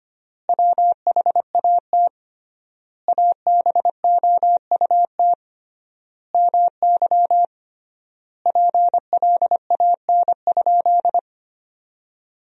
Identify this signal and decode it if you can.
Morse code, heard as a keyed tone, about 25 wpm, 700 Hz. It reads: WHAT ABOUT MY PLAN?
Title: 3x Word-Spacing - Sentences from Top 500 Words - 25wpm